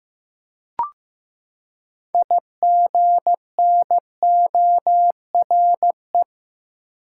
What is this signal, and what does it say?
Morse code, heard as a keyed tone, about 15 wpm, 700 Hz. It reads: IGNORE
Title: Six Letter Words - 15wpm